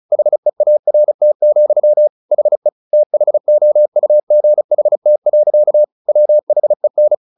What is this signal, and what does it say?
Morse code, heard as a keyed tone, about 34 wpm, 600 Hz. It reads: HEART, HE THOUGHT. WHEN
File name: the old man and the sea 2 35wpm